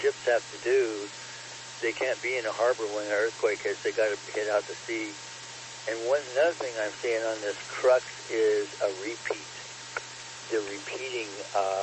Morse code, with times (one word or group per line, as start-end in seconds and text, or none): none